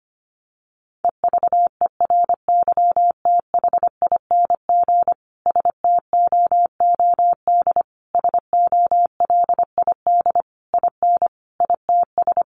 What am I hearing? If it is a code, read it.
Morse code, heard as a keyed tone, about 25 wpm, 700 Hz. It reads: EVERYTHING STOOD SOLID IN ITS
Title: Sentences from Top 1500 Words 25wpm